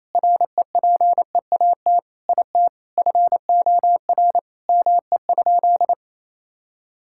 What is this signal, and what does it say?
Morse code, heard as a keyed tone, about 28 wpm, 700 Hz. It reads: REPEAT IT FOR ME?
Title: Sentences from Top 1600 Words 28wpm